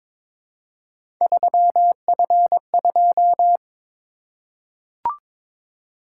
Morse code, heard as a keyed tone, about 22 wpm, 700 Hz. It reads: 3F2
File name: practice / 22wpm / Three Letter-Number Combinations - 22wpm